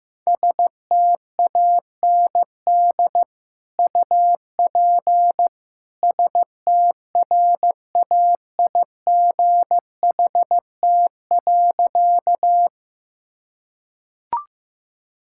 Morse code, heard as a keyed tone, about 16 wpm, 700 Hz. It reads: STAND UP STRAIGHT.
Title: Sentences from Top 1000 Words - Review Part 6 - 15wpm